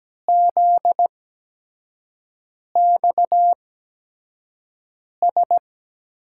Morse code, heard as a keyed tone, about 17 wpm, spaced as for 5 wpm, 700 Hz. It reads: ZXS
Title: USA Call Signs - 6 Characters - 8x Character Spacing - 17wpm